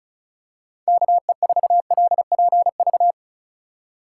Morse code, heard as a keyed tone, about 36 wpm, 700 Hz. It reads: KE4LPV
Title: USA Call Signs - 4x Character Spacing - 35wpm